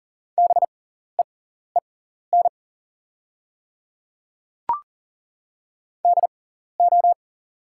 Morse code, heard as a keyed tone, about 40 wpm, spaced as for 14 wpm, 700 Hz. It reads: BEEN DO
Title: 6x Character Spacing - Top 100 Words - 40wpm